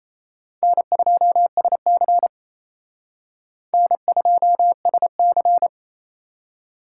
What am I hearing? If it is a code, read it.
Morse code, heard as a keyed tone, about 35 wpm, 700 Hz. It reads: N2SC N2SC
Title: Speed Racing - USA 1x2 Call Signs 28wpm